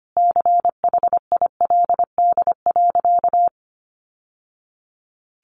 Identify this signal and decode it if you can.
Morse code, heard as a keyed tone, about 25 wpm, 700 Hz. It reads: CHILD.
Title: Sentences from Top 800 Words 25wpm